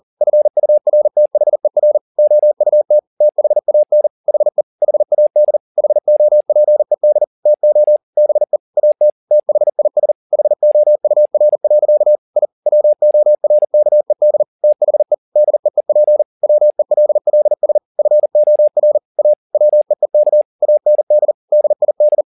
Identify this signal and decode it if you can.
Morse code, heard as a keyed tone, about 40 wpm, 600 Hz. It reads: FURTHER OUT THAN HE HAD HOPED TO BE AT THIS HOUR. I WORKED THE DEEP WELLS FOR A WEEK AND DID